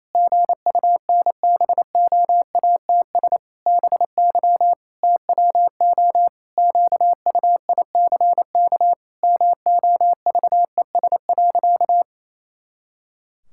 Morse code, heard as a keyed tone, about 28 wpm, 700 Hz. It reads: GUNBOATS BY TWO QUICK MOVES.
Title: Pangram Challenge v2 - 28wpm